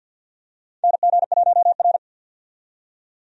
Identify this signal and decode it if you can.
Morse code, heard as a keyed tone, about 46 wpm, 700 Hz. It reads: NG1R
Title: USA Call Signs - 4 Characters - 50wpm